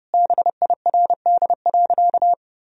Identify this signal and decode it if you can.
Morse code, heard as a keyed tone, about 30 wpm, 700 Hz. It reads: BIRD.
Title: Sentences from Top 1000 Words - Review Part 2 - 30wpm